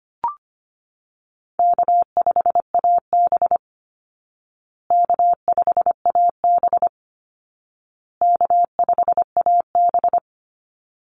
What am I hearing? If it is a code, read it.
Morse code, heard as a keyed tone, about 25 wpm, 700 Hz. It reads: K5AB K5AB K5AB
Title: Easy Call Signs Repeated 3x - 25wpm